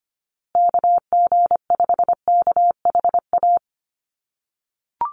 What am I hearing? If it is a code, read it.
Morse code, heard as a keyed tone, about 25 wpm, 700 Hz. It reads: KG5KHA
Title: USA Call Signs Repeated 3x 25wpm